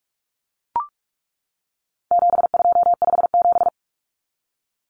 Morse code, heard as a keyed tone, about 44 wpm, 700 Hz. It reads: 7257
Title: Numbers - 4-Digit 45wpm